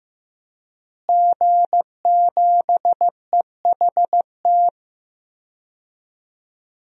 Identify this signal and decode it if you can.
Morse code, heard as a keyed tone, about 16 wpm, 700 Hz. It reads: G7EHT
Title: DXCC-100 Call Signs - 5 Characters - Repeated 2x - 15wpm